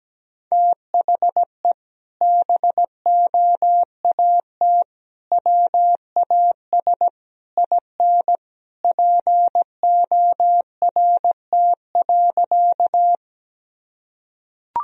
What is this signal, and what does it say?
Morse code, heard as a keyed tone, about 17 wpm, 700 Hz. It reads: THE BOAT WAS IN PORT.
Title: Sentences from Top 1000 Words - Review Part 1 - 17wpm